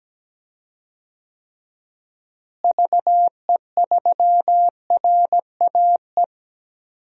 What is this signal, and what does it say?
Morse code, heard as a keyed tone, about 17 wpm, 700 Hz. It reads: VE3RAE
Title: DXCC-100 Call Signs - 6 Characters - Repeated 2x - 17wpm